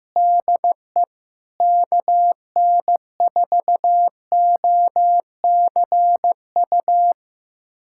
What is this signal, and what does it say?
Morse code, heard as a keyed tone, about 15 wpm, 700 Hz. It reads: DE KN4OCU
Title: USA Callsign DE Callsign - 6 Characters - 15wpm